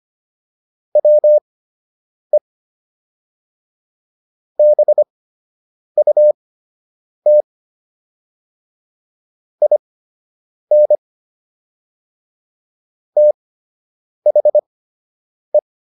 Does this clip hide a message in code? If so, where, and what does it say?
Yes: Morse code from 1.0 to 15.6 s, heard as a keyed tone, about 25 wpm, spaced as for 8 wpm, 600 Hz. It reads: WE BUT IN THE